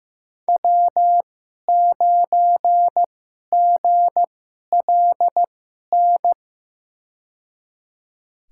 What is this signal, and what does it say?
Morse code, heard as a keyed tone, about 15 wpm, spaced as for 11 wpm, 700 Hz. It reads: W9GLN
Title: USA Call Signs - 5 Characters - 2x Character Spacing - 15wpm